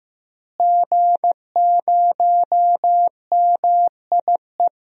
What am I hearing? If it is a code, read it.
Morse code, heard as a keyed tone, about 15 wpm, 700 Hz. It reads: G0MIE
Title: DXCC-100 Call Signs - 5 Characters - Repeated 3x - 15wpm